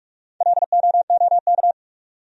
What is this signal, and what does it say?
Morse code, heard as a keyed tone, about 48 wpm, 700 Hz. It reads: LOOK
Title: Top 100 Words 45wpm